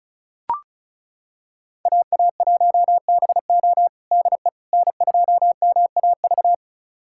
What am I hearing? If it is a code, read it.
Morse code, heard as a keyed tone, about 36 wpm, 700 Hz. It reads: AA1BO DE N2MAV
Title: USA Callsign DE Callsign - 5 Characters - 35wpm